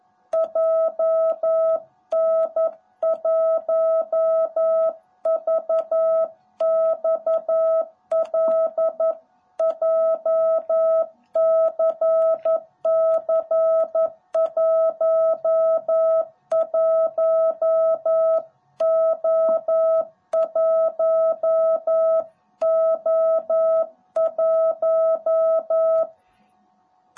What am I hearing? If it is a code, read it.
Morse code, heard as a keyed tone, about 11 wpm, 660 Hz. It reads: JN1VXL JCC11O1O1
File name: jh8cbh / 2021cw / cw18b